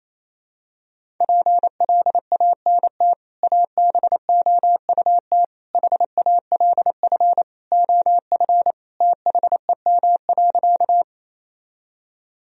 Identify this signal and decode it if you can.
Morse code, heard as a keyed tone, about 28 wpm, 700 Hz. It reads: PLANT ABOUT HALF OF THEM.
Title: Sentences from Top 1000 Words - Review Part 1 - 28wpm